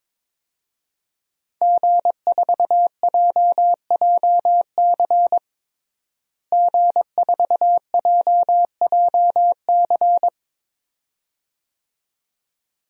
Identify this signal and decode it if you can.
Morse code, heard as a keyed tone, about 22 wpm, 700 Hz. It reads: G4JJC G4JJC
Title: DXCC-100 Call Signs - 5 Characters - Repeated 2x - 22wpm